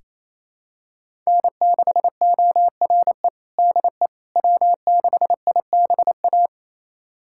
Morse code, heard as a keyed tone, about 28 wpm, 700 Hz. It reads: N6ORE DE W6IBA
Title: USA Callsign DE Callsign - 5 Characters - 28wpm